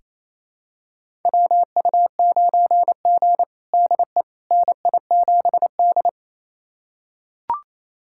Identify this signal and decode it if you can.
Morse code, heard as a keyed tone, about 28 wpm, 700 Hz. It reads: WU9G DE NI7D
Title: USA Callsign DE Callsign - 4 Characters - 28wpm